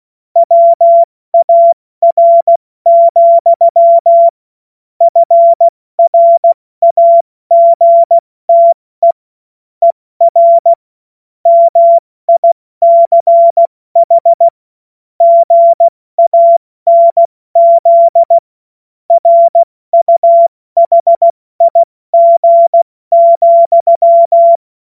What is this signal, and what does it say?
Morse code, heard as a keyed tone, about 16 wpm, 680 Hz. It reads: WAR, FRAGTE ER MICH GANZ RUHIG,
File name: Morsekurs / Witze / Witz 29. Juni 2021, 70 F80